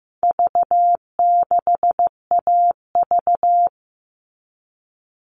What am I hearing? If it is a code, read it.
Morse code, heard as a keyed tone, about 15 wpm, 700 Hz. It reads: V6AV
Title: All Cut Numbers - 4-Digit 15wpm